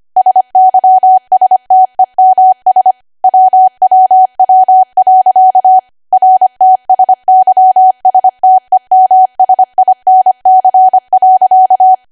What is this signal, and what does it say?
Morse code, heard as a keyed tone, about 25 wpm, 750 Hz. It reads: SYSTEMS WWW. RTSYSTEMSINC.
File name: Archive / 25 WPM / 25WPM